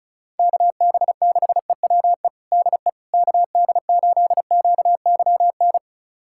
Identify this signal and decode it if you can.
Morse code, heard as a keyed tone, about 36 wpm, 700 Hz. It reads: KB6EWE DE KD8QYN
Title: USA Callsign DE Callsign - 6 Characters - 35wpm